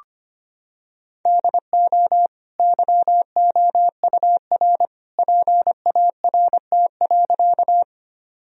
Morse code, heard as a keyed tone, about 25 wpm, 700 Hz. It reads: DO YOUR PART.
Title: Sentences from Top 1000 Words - Review Part 1 - 25wpm